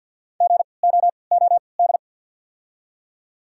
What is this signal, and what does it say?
Morse code, heard as a keyed tone, about 52 wpm, spaced as for 29 wpm, 700 Hz. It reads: GOOD
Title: 3x Character Spacing - Top 200 Words - 50wpm